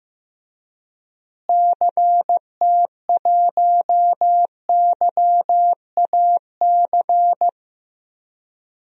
Morse code, heard as a keyed tone, about 15 wpm, 700 Hz. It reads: CT1YAC
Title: DXCC-100 Call Signs - 6 Characters - Repeated 2x - 15wpm